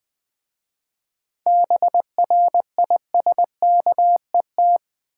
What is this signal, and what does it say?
Morse code, heard as a keyed tone, about 20 wpm, 690 Hz. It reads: BRISKET
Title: Difficult Words with Hints - 20wpm